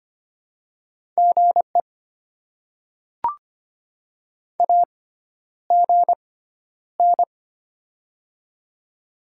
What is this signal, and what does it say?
Morse code, heard as a keyed tone, about 25 wpm, spaced as for 11 wpm, 700 Hz. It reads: GE AGN
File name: Top 20 QSO Elements - 6x Character Spacing - 25wpm